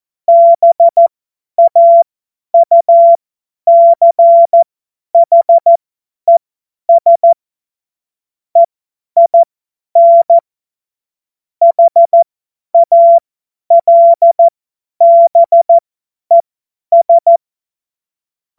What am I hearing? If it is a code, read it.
Morse code, heard as a keyed tone, about 14 wpm, spaced as for 10 wpm, 680 Hz. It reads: BAUCHES EIN HALBES
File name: Morsekurs / Witze / Witz 07.November 2023 50 F70